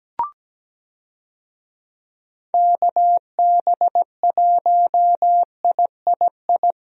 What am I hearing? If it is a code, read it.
Morse code, heard as a keyed tone, about 17 wpm, 700 Hz. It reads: KB1III
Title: USA Call Signs - 6 Characters - Repeated 3x - 17wpm